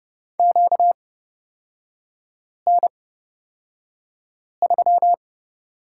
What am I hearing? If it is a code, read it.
Morse code, heard as a keyed tone, about 30 wpm, spaced as for 5 wpm, 690 Hz. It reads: QN3
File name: Lesson 38e - Letter q - Three Characters 30wpm